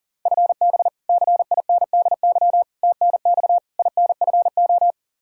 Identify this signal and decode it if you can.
Morse code, heard as a keyed tone, about 40 wpm, 700 Hz. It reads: FB CINDY TNX INFO
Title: Practice QSO - Exchange 3 - More Info - 40wpm